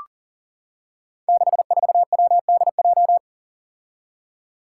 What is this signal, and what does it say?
Morse code, heard as a keyed tone, about 40 wpm, 700 Hz. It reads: 64WDJ
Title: Five Challenging Letter-Number Combinations 40wpm